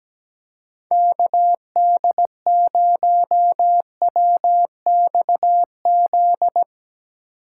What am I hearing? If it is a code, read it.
Morse code, heard as a keyed tone, about 17 wpm, 700 Hz. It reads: KD0WXZ
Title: USA Call Signs - 6 Characters - 4x Character Spacing - 17wpm